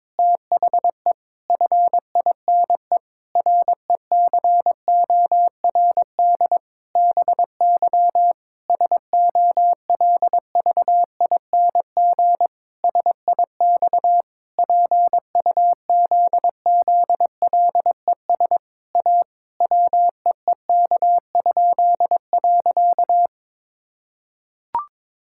Morse code, heard as a keyed tone, about 22 wpm, 700 Hz. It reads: THE FINE RECORD BY SOLVING SIX PUZZLES A WEEK?.
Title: Pangram Challenge v2 - 22wpm